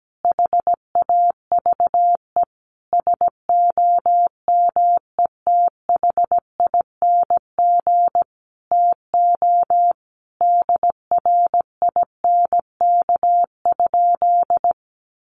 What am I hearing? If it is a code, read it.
Morse code, heard as a keyed tone, about 17 wpm, 700 Hz. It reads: HAVE SOMETHING TO DRINK?
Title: Sentences from Top 1500 Words 17wpm